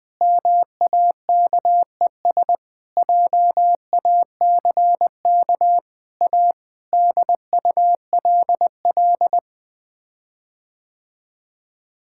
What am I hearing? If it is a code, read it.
Morse code, heard as a keyed tone, about 20 wpm, 700 Hz. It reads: MAKES JACK A DULL
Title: Sets of 8 Words - 20wpm